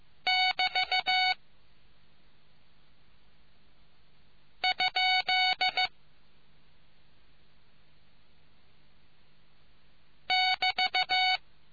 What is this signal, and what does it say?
Morse code, heard as a keyed tone, about 15 wpm, 2250 Hz. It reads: <BT> ? <BT>